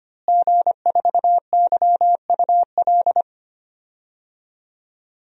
Morse code, heard as a keyed tone, about 25 wpm, 700 Hz. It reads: G4YUL
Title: DXCC-100 Call Signs - 5 Characters - Repeated 3x - 25wpm